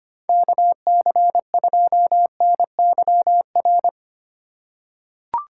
Morse code, heard as a keyed tone, about 25 wpm, 700 Hz. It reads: KC2NYR
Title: USA Call Signs - 8x Character Spacing - 25wpm